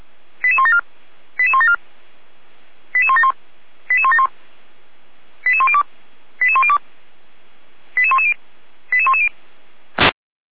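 Five-tone selective call (ZVEI), heard as quick runs of five tones, tones of about 80 ms. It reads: 88174 88174 88171 88171 88102 88102 88110 88110
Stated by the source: ZVEI five-tone selective call 88012 received on SDR, archived by feuerwehr-orlamuende fire brigade server.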